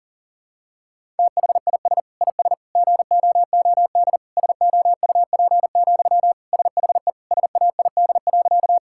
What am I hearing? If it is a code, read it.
Morse code, heard as a keyed tone, about 40 wpm, 700 Hz. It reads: THIS IS GOOD SOUP, SHE SAID.
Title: Sentences from Top 1000 Words - Review Part 6 - 40wpm